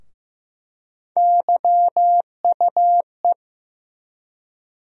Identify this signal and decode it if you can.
Morse code, heard as a keyed tone, about 15 wpm, 700 Hz. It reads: YUE
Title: USA Call Sign Postfixes - 3 Letters 15wpm